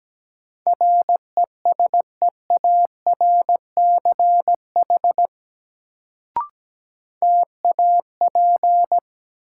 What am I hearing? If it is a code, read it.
Morse code, heard as a keyed tone, about 17 wpm, 700 Hz. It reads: RESEARCH TAP
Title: Sentences from Top 1400 Words - New Word Drill Alt - 17wpm